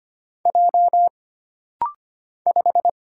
Morse code, heard as a keyed tone, about 25 wpm, 700 Hz. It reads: J 5
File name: Single Character (Letter:Number:Punctuation:Prosign) - Rapid-Fire - 25wpm